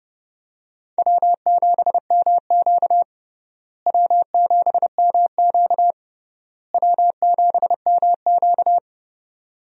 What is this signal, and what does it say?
Morse code, heard as a keyed tone, about 30 wpm, 700 Hz. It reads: W7MQ W7MQ W7MQ
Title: USA Call Signs - 4 Characters - Repeated 3x - 30wpm